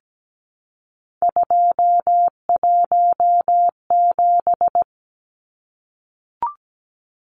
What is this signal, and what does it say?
Morse code, heard as a keyed tone, about 17 wpm, 700 Hz. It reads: 217
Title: Numbers - 3-Digit 17wpm